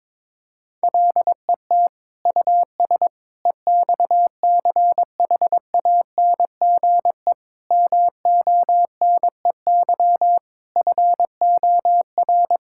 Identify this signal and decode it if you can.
Morse code, heard as a keyed tone, about 22 wpm, 700 Hz. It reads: LET US EXCHANGE MONEY FOR
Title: Sentences from Top 1000 Words - Review Part 6 - 22wpm